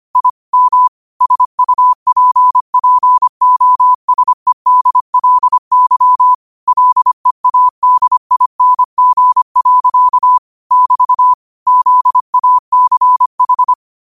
Morse code, heard as a keyed tone, about 25 wpm, 1000 Hz. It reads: I M SUPPOSEDLY LEADING. <BT> ZACH